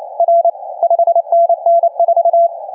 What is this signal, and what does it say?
Morse code, heard as a keyed tone, about 30 wpm, 720 Hz. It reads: R 5NN 4
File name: cqww cw 10 minutes